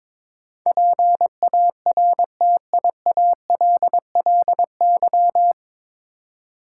Morse code, heard as a keyed tone, about 22 wpm, 700 Hz. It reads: PARTIALLY
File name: 9 Letter Words - First 3 Letters Given - 22wpm